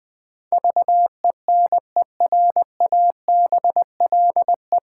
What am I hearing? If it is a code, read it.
Morse code, heard as a keyed tone, about 20 wpm, 700 Hz. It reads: VENERABLE
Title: Pangram Challenge v2 - 20wpm